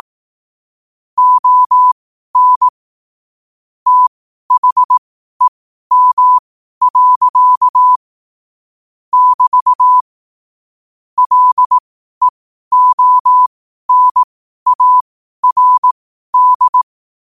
Morse code, heard as a keyed tone, about 18 wpm, spaced as for 12 wpm, 1000 Hz. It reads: ON THEM. <BT> LEONARD
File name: qotdfiles / 12wpm